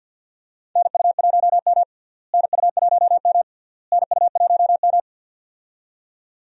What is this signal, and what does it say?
Morse code, heard as a keyed tone, about 48 wpm, 700 Hz. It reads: NU1M NU1M NU1M
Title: USA Call Signs - 4 Characters - Repeated 3x - 50wpm